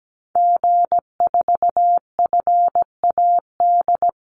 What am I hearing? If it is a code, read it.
Morse code, heard as a keyed tone, about 17 wpm, 700 Hz. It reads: G4FAD
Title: Call Signs Repeated 3x - 17wpm